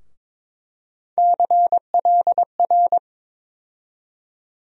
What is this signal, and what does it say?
Morse code, heard as a keyed tone, about 22 wpm, 700 Hz. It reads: CLR
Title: USA Call Sign Postfixes - 3 Letters 22wpm